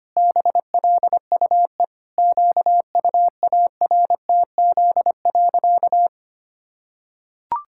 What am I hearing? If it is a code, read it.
Morse code, heard as a keyed tone, about 25 wpm, 700 Hz. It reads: BLUE QUARTZ.
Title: Pangram Challenge v2 - 25wpm